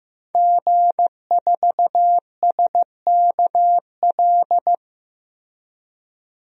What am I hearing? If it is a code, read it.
Morse code, heard as a keyed tone, about 15 wpm, 700 Hz. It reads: G4SKL
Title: DXCC-100 Call Signs - 5 Characters - Repeated 3x - 15wpm